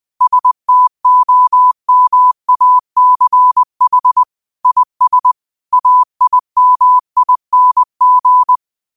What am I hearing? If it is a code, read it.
Morse code, heard as a keyed tone, about 20 wpm, 1000 Hz. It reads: STOMACH IS AIMING
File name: qotdfiles / 20wpm